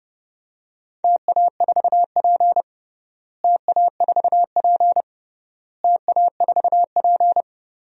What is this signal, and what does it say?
Morse code, heard as a keyed tone, about 30 wpm, 700 Hz. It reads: TA4P TA4P TA4P
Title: DXCC-100 Call Signs - 4 Characters - Repeated 3x - 30wpm